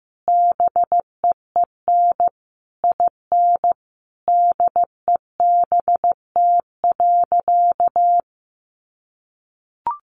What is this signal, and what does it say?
Morse code, heard as a keyed tone, about 15 wpm, 700 Hz. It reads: BEEN IN DEBT.
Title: Sentences from Top 1300 Words 15wpm